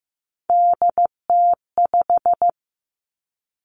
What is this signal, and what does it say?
Morse code, heard as a keyed tone, about 15 wpm, 700 Hz. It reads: DT5
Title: Three Challenging Letter-Number Combinations 15wpm